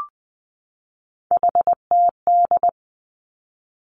Morse code, heard as a keyed tone, about 20 wpm, 700 Hz. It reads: HTD
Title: Three Challenging Letter-Number Combinations 20wpm